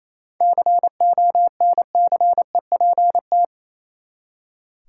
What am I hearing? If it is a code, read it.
Morse code, heard as a keyed tone, about 28 wpm, 700 Hz. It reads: CONCEPT
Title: Seven Letter Words - 28wpm